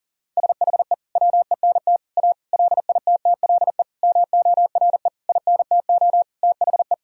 Morse code, heard as a keyed tone, about 40 wpm, 700 Hz. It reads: SHE WENT A LITTLE MORE INTO THE
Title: Sentences from Top 1000 Words - Review Part 2 - 40wpm